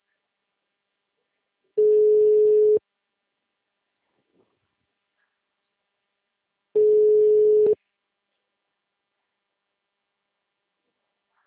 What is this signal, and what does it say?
Ringback tone (Europe, 425 Hz).
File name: out- -2015- - - .5489